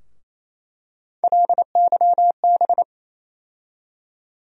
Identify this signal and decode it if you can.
Morse code, heard as a keyed tone, about 28 wpm, 700 Hz. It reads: LYB
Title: USA Call Sign Postfixes - 3 Letters 28wpm